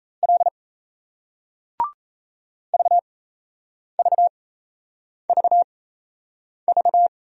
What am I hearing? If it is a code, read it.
Morse code, heard as a keyed tone, about 36 wpm, spaced as for 8 wpm, 700 Hz. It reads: L VVVV